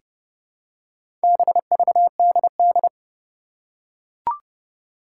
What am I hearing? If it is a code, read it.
Morse code, heard as a keyed tone, about 29 wpm, 690 Hz. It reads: BVDD
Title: Four Challenging Letter-Number Combinations 30wpm